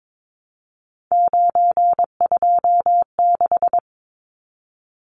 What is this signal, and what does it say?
Morse code, heard as a keyed tone, about 22 wpm, 700 Hz. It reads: 926
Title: Numbers - 3-Digit 22wpm